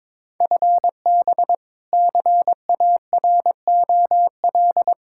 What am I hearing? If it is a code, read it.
Morse code, heard as a keyed tone, about 22 wpm, 700 Hz. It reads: FB CAROL
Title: K1USN SST QSO Practice 22wpm